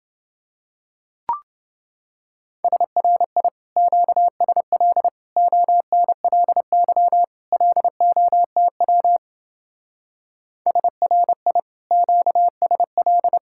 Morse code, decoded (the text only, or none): SRI QSL ONLY LOTW SRI QSL